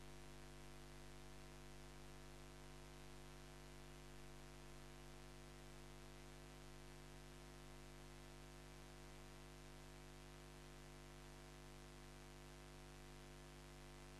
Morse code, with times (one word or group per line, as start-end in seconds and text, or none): none